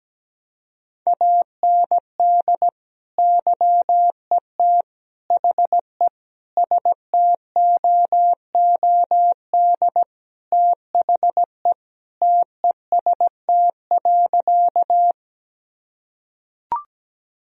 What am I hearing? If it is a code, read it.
Morse code, heard as a keyed tone, about 17 wpm, 700 Hz. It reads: AND YET HE STOOD THE TEST.
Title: Sentences from Top 1000 Words - Review Part 3 - 17wpm